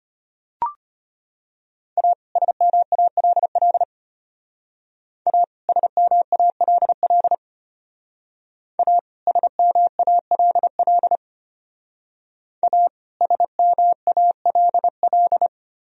Morse code, heard as a keyed tone, about 30 wpm, 700 Hz. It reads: A SMALL A SMALL A SMALL A SMALL
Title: Speed-Racing - Sets of 2 Words 25wpm